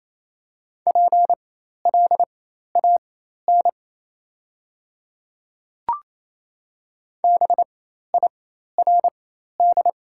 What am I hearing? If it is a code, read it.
Morse code, heard as a keyed tone, about 28 wpm, spaced as for 13 wpm, 700 Hz. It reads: PLAN BIRD
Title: 4x Character Spacing - Top 500 Words - 28wpm